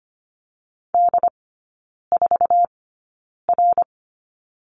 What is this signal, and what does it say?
Morse code, heard as a keyed tone, about 25 wpm, spaced as for 9 wpm, 700 Hz. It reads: D4R
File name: course / 25:10wpm / 093 Lesson 16e - Letter c - Three Characters 25wpm